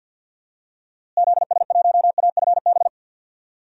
Morse code, heard as a keyed tone, about 51 wpm, 700 Hz. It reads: ZS1AFB
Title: DXCC-100 Call Signs - 6 Characters - Repeated 3x - 50wpm